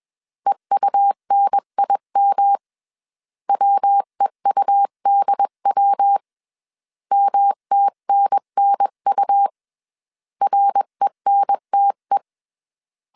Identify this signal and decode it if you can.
Morse code, heard as a keyed tone, about 21 wpm, 800 Hz. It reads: EUNIM WEUDW MTNNU RENTE